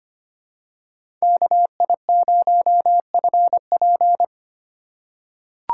Morse code, heard as a keyed tone, about 25 wpm, 700 Hz. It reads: KI0FP
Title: USA Call Signs - 5 Characters - 2x Character Spacing - 25wpm